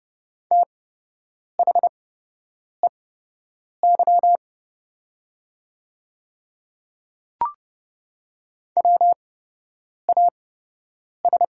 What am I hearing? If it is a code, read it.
Morse code, heard as a keyed tone, about 30 wpm, spaced as for 8 wpm, 700 Hz. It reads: THEY WAS
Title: 8x Character Spacing - Top 100 Words - 30wpm